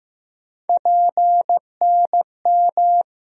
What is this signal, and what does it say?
Morse code, heard as a keyed tone, about 15 wpm, 700 Hz. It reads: PNM
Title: USA Call Sign Postfixes - 3 Letters 15wpm